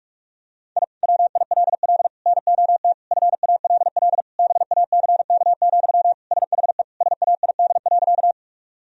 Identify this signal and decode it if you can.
Morse code, heard as a keyed tone, about 41 wpm, 700 Hz. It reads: I WILL NOT FALL BACK, SHE SAID.
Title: Sentences from Top 1000 Words - Review Part 2 - 45wpm